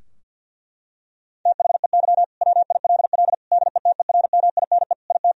Morse code, heard as a keyed tone, about 50 wpm, 700 Hz. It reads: THEY WILL DETERMINE IT
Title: Sentences from Top 1000 Words - Review Part 4 - 50wpm